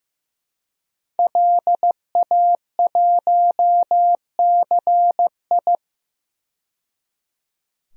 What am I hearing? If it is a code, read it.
Morse code, heard as a keyed tone, about 15 wpm, 700 Hz. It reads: LA1CI
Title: DXCC-100 Call Signs - 5 Characters - 15wpm